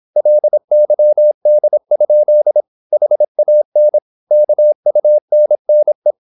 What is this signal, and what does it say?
Morse code, heard as a keyed tone, about 26 wpm, 600 Hz. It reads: LYD? HAN KUNNE
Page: Ferie 26 wpm